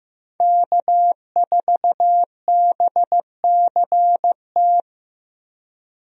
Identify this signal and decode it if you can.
Morse code, heard as a keyed tone, about 15 wpm, 700 Hz. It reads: K4BCT
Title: USA Call Signs - 5 Characters - Repeated 3x - 15wpm